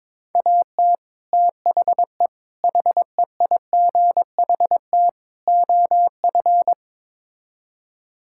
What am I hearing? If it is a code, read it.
Morse code, heard as a keyed tone, about 22 wpm, 700 Hz. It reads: AT THE HEIGHT OF
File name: practice / 22wpm / Build A Phrase - 5-Word Phrases - 22wpm